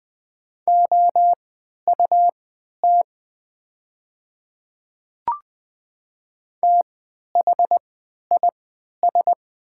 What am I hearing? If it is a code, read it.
Morse code, heard as a keyed tone, about 20 wpm, spaced as for 11 wpm, 700 Hz. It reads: OUT THIS
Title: 3x Character Spacing - Top 200 Words - 20wpm